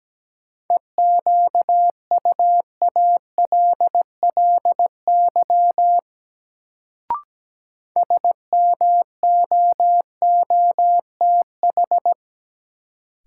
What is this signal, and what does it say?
Morse code, heard as a keyed tone, about 17 wpm, 700 Hz. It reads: EQUALLY SMOOTH
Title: Sentences from Top 1400 Words - New Word Drill - 17wpm